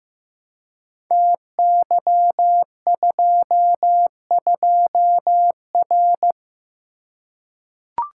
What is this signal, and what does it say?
Morse code, heard as a keyed tone, about 15 wpm, 700 Hz. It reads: TY22R
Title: Code Groups - 5 Letter:Numbers - 15wpm